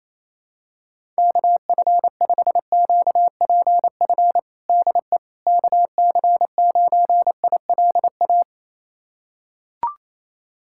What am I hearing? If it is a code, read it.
Morse code, heard as a keyed tone, about 28 wpm, 700 Hz. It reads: KF5QPF DE KC9ILA
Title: USA Callsign DE Callsign 28wpm